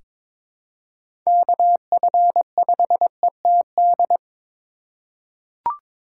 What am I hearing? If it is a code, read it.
Morse code, heard as a keyed tone, about 22 wpm, 700 Hz. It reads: KF5ETD
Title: K1USN SST QSO Practice - 22wpm